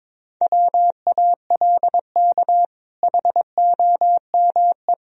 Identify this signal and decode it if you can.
Morse code, heard as a keyed tone, about 21 wpm, 700 Hz. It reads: WALK HOME
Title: Sentences from Top 1000 Words - Review Part 1 - 22wpm